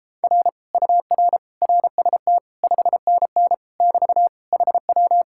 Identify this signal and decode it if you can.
Morse code, heard as a keyed tone, about 33 wpm, 700 Hz. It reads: R UR RST 5NN <BT> HW